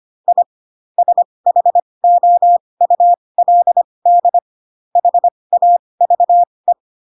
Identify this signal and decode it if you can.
Morse code, heard as a keyed tone, about 25 wpm, spaced as for 20 wpm, 700 Hz. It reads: I SHOULD HAVE